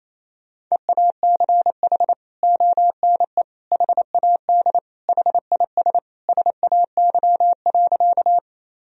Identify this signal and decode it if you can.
Morse code, heard as a keyed tone, about 28 wpm, 700 Hz. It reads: EACH ONE HAD HIS SAY.
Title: Sentences from Top 1000 Words - Review Part 1 - 28wpm